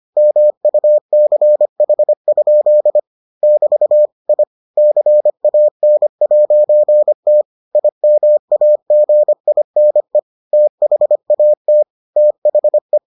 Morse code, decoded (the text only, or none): MUCH? <BT> I CAN'T IMAGINE THAT THE